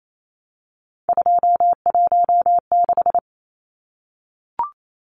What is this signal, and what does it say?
Morse code, heard as a keyed tone, about 28 wpm, 700 Hz. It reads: 216
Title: Numbers - 3-Digit 28wpm